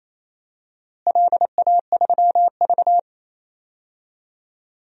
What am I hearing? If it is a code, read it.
Morse code, heard as a keyed tone, about 28 wpm, 700 Hz. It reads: LA3V
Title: DXCC-100 Call Signs - 4 Characters - Repeated 3x - 28wpm